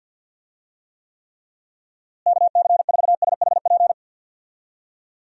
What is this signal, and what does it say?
Morse code, heard as a keyed tone, about 52 wpm, 700 Hz. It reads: KC4SHP
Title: USA Call Signs - 6 Characters - Repeated 3x - 50wpm